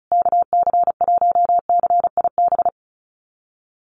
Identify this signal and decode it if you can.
Morse code, heard as a keyed tone, about 34 wpm, 700 Hz. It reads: KC1CIB
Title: USA Call Signs Repeated 3x 35wpm